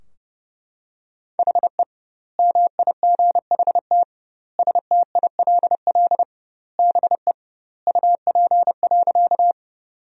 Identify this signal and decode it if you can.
Morse code, heard as a keyed tone, about 30 wpm, 700 Hz. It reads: HE MIGHT STILL BE UP.
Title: 2x Word-Spacing - Sentences from Top 300 Words - 30wpm